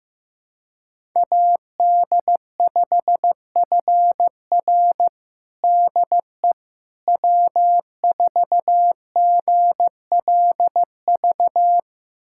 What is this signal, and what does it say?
Morse code, heard as a keyed tone, about 15 wpm, 700 Hz. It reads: AD5FR DE W4GLV
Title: USA Callsign DE Callsign - 5 Characters - 15wpm